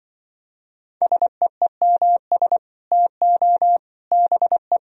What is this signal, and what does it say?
Morse code, heard as a keyed tone, about 25 wpm, 700 Hz. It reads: SEEMS TO BE
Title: Speed-Racing - Sets of 3 Words 20wpm